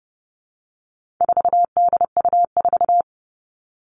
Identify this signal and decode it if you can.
Morse code, heard as a keyed tone, about 30 wpm, 700 Hz. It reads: 4DU4
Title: All Cut Numbers - 4-Digit 30wpm